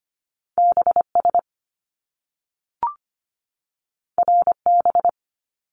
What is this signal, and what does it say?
Morse code, heard as a keyed tone, about 25 wpm, 700 Hz. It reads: BS RB
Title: Top Two Letter Combinations in Words 25wpm